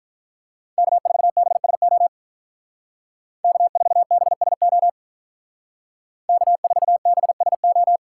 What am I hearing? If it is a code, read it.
Morse code, heard as a keyed tone, about 43 wpm, 700 Hz. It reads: K4BSO K4BSO K4BSO
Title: Speed Racing - USA 1x3 Call Signs 35wpm